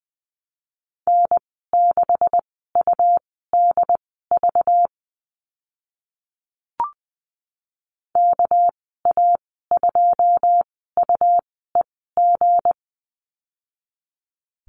Morse code, heard as a keyed tone, about 20 wpm, spaced as for 14 wpm, 700 Hz. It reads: N6UDV KA2UEG
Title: 238 Lesson 38g - Letter q - Call Signs 20wpm